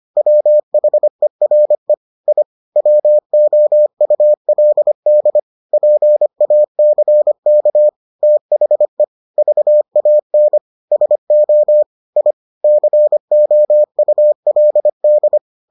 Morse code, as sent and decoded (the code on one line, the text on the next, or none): .-- .... .-. / .. / .-- --- ..- .-.. -.. / .--. .- -.-. -.- / - .... / ...- .- -. / ... --- / .. / -.-. --- ..- .-.. -..
WHERE I WOULD PACK THE VAN SO I COULD